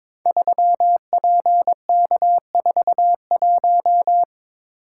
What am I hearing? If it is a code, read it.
Morse code, heard as a keyed tone, about 22 wpm, 700 Hz. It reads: 3PK41
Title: Code Groups - 5 Letter:Numbers - 22wpm